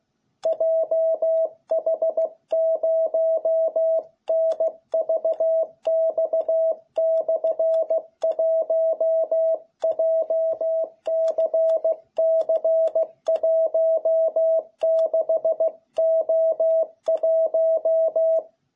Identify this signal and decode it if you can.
Morse code, heard as a keyed tone, about 16 wpm, 660 Hz. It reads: JH0NVX/1 JCC16O1